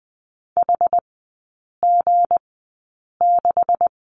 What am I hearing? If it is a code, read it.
Morse code, heard as a keyed tone, about 20 wpm, spaced as for 8 wpm, 700 Hz. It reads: HG6
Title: 172 Lesson 28e - Number 9 - Three Characters 20wpm